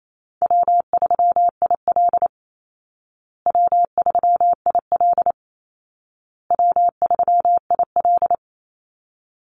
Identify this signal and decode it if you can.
Morse code, heard as a keyed tone, about 28 wpm, 700 Hz. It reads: W3IL W3IL W3IL
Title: Easy Call Signs Repeated 3x - 28wpm